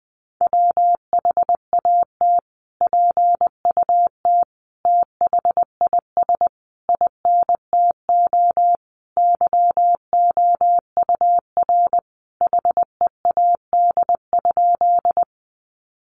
Morse code, heard as a keyed tone, about 20 wpm, 700 Hz. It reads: WHAT PUT THIS INTO YOUR HEAD?
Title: Sentences from Top 600 Words 20wpm